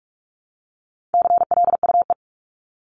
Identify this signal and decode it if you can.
Morse code, heard as a keyed tone, about 44 wpm, 700 Hz. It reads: CLUE
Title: Top 1751-2000 Words 45wpm